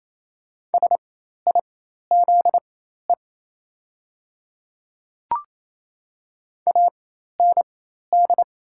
Morse code, heard as a keyed tone, about 28 wpm, spaced as for 13 wpm, 700 Hz. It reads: SIZE AND
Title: 4x Character Spacing - Top 500 Words - 28wpm